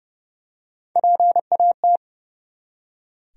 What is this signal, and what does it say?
Morse code, heard as a keyed tone, about 32 wpm, 700 Hz. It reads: PAT
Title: USA Common Names - Max 3 Letters - 30wpm